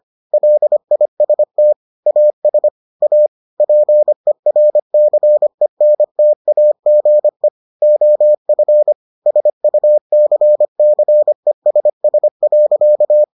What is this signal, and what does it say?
Morse code, heard as a keyed tone, about 25 wpm, 600 Hz. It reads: LIST AS A PERCENTAGE OF SUCCESS.